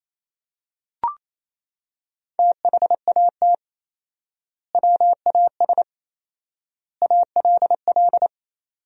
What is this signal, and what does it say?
Morse code, heard as a keyed tone, about 28 wpm, 700 Hz. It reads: THAT WAS ALL
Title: 4x Word-Spacing - Sentences from Top 100 Words - 28wpm